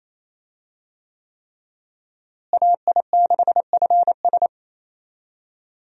Morse code, heard as a keyed tone, about 28 wpm, 700 Hz. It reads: AI6FS
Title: USA Call Signs - 5 Characters - Repeated 3x - 28wpm